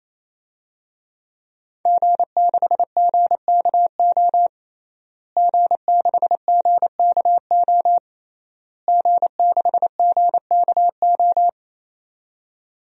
Morse code, heard as a keyed tone, about 28 wpm, 700 Hz. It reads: G6GKO G6GKO G6GKO
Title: DXCC-100 Call Signs - 5 Characters - Repeated 3x - 28wpm